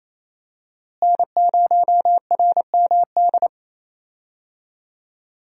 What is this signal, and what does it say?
Morse code, heard as a keyed tone, about 28 wpm, 700 Hz. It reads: N0RMD
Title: USA Call Signs - 5 Characters - Repeated 3x - 28wpm